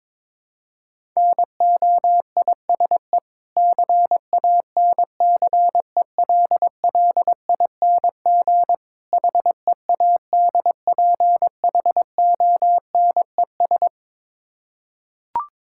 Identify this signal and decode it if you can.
Morse code, heard as a keyed tone, about 22 wpm, 690 Hz. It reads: NOISE CANCELLING HEADPHONES
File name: Difficult Words with Hints - 22wpm